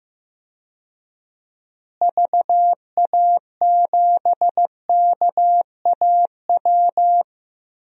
Morse code, heard as a keyed tone, about 15 wpm, 700 Hz. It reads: VA7KAW
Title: DXCC-100 Call Signs - 6 Characters - Repeated 2x - 15wpm